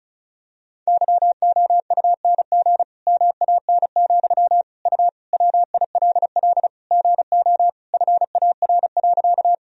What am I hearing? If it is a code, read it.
Morse code, heard as a keyed tone, about 36 wpm, 700 Hz. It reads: YOUNG MAN, U WILL GO FAR.
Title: Sentences from Top 400 Words and Abbreviations 35wpm